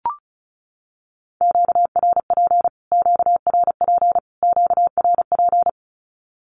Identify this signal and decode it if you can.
Morse code, heard as a keyed tone, about 34 wpm, 700 Hz. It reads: QRP QRP QRP
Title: Q-Codes Repeated 3x 35wpm